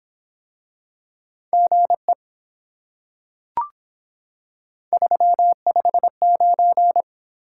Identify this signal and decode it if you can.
Morse code, heard as a keyed tone, about 25 wpm, 700 Hz. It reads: GE 359